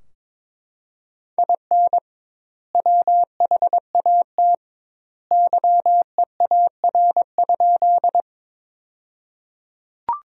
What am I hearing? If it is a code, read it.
Morse code, heard as a keyed tone, about 22 wpm, 700 Hz. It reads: IN WHAT YEAR?
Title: 2x Word-Spacing - Sentences from Top 200 Words - 22wpm